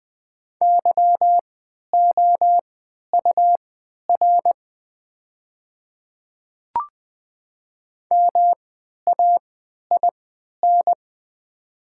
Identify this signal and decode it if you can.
Morse code, heard as a keyed tone, about 20 wpm, spaced as for 11 wpm, 690 Hz. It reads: YOUR MAIN
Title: 3x Character Spacing - Top 300 Words - 20wpm